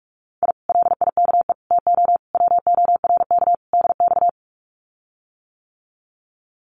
Morse code, heard as a keyed tone, about 46 wpm, 700 Hz. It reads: I LIKE TO WORK DX
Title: Common QSO Phrases Encore 45wpm